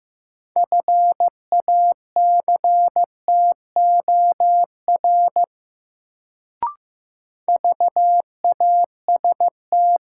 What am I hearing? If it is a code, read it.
Morse code, heard as a keyed tone, about 15 wpm, 700 Hz. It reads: FACTOR VAST
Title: Sentences from Top 1300 Words - New Word Drill - 15wpm